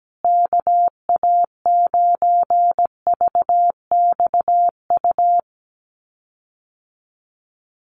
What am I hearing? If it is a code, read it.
Morse code, heard as a keyed tone, about 17 wpm, 700 Hz. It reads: KA9VXU
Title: USA Call Signs Repeated 2x 17wpm